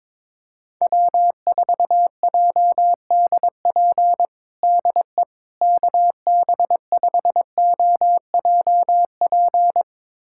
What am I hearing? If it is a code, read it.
Morse code, heard as a keyed tone, about 22 wpm, 700 Hz. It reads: W4JDP DE KB5OJP
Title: USA Callsign DE Callsign 22wpm